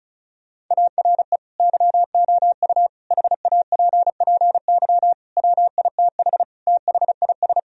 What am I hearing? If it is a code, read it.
Morse code, heard as a keyed tone, about 36 wpm, 700 Hz. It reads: ARE YOU HAPPY WITH THIS
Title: Sentences from Top 1000 Words - Review Part 2 - 35wpm